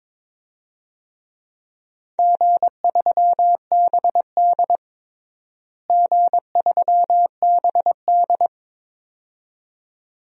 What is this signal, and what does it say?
Morse code, heard as a keyed tone, about 21 wpm, 700 Hz. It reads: G3BD G3BD
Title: DXCC-100 Call Signs - 4 Characters - Repeated 2x - 22wpm